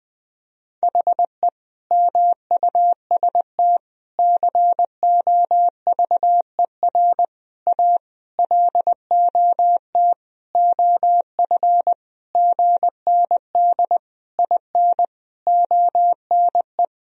Morse code, heard as a keyed tone, about 20 wpm, 700 Hz. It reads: HE MUST COVER A LOT OF GND IN ONE
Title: Sentences from Top 500 Words and Abbreviations 20wpm